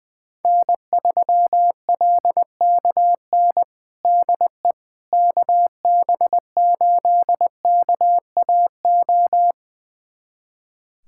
Morse code, heard as a keyed tone, about 20 wpm, 700 Hz. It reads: N3LKN DE KB8KAO
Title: USA Callsign DE Callsign 20wpm